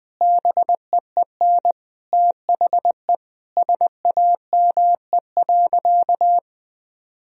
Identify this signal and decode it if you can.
Morse code, heard as a keyed tone, about 20 wpm, 700 Hz. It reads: BEEN THE SAME.
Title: Sentences from Top 1000 Words - Review Part 1 - 20wpm